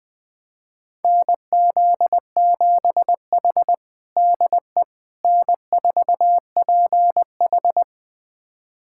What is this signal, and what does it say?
Morse code, heard as a keyed tone, about 20 wpm, 700 Hz. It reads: NZ7H DE N4PH
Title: USA Callsign DE Callsign - 4 Characters - 20wpm